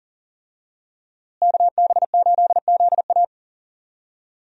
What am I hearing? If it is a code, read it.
Morse code, heard as a keyed tone, about 40 wpm, 700 Hz. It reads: KB8ZA
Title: USA Call Signs - 5 Characters - Repeated 3x - 40wpm